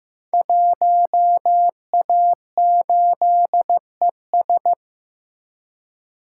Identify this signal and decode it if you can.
Morse code, heard as a keyed tone, about 15 wpm, 700 Hz. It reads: 1A8ES
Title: Code Groups - 5 Letter:Numbers - 15wpm